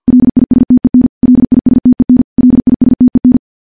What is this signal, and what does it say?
Morse code, heard as a keyed tone, about 50 wpm, 250 Hz. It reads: LISTEN LISTEN LISTEN
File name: morse / morse-training-50WPM